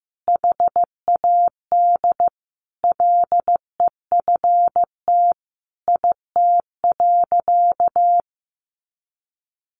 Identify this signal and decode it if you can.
Morse code, heard as a keyed tone, about 15 wpm, 700 Hz. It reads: HAD LEFT IT.
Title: A Sound Of Thunder 15wpm